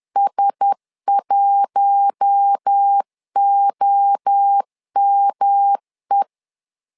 Morse code, heard as a keyed tone, about 10 wpm, 800 Hz. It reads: S1OME
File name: LEZIONI / 4 Q1S 10wpm 2133